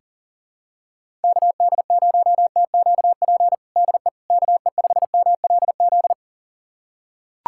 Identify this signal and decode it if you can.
Morse code, heard as a keyed tone, about 37 wpm, 700 Hz. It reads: KD0TQP DE KE5MLZ
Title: USA Callsign DE Callsign - 6 Characters - 40wpm